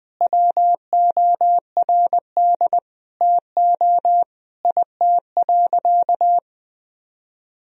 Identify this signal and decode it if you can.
Morse code, heard as a keyed tone, about 20 wpm, 700 Hz. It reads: WORD TO IT.
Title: Sentences from Top 1000 Words - Review Part 1 - 20wpm